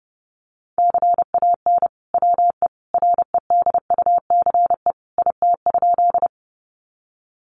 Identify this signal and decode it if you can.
Morse code, heard as a keyed tone, about 30 wpm, 700 Hz. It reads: CAN WE REDUCE IT?
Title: Sentences from Top 1100 Words 30wpm